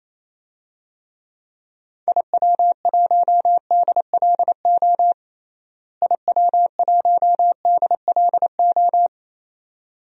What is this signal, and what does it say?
Morse code, heard as a keyed tone, about 27 wpm, 700 Hz. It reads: IW1DLO IW1DLO
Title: DXCC-100 Call Signs - 6 Characters - Repeated 2x - 28wpm